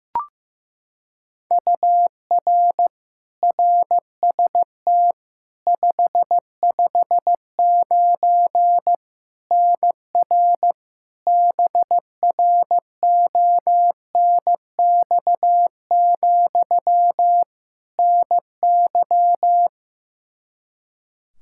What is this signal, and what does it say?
Morse code, heard as a keyed tone, about 15 wpm, 700 Hz. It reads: UR RST 559 NR BRONX, NY
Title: Practice QSO - Exchange 1 - Beginning - 15wpm